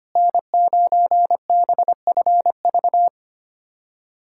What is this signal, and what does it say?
Morse code, heard as a keyed tone, about 25 wpm, 700 Hz. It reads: N9BFV
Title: USA Callsign DE Callsign - 5 Characters - 25wpm